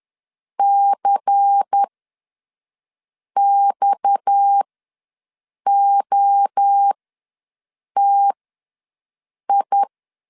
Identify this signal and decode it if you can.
Morse code, heard as a keyed tone, about 10 wpm, spaced as for 5 wpm, 800 Hz. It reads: CXOTI
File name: LEZIONI / 9 28 ws 10wpm 4124